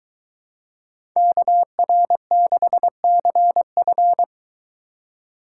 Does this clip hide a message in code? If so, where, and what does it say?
Yes: Morse code from 1.2 to 4.2 s, heard as a keyed tone, about 23 wpm, 700 Hz. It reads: KR6CF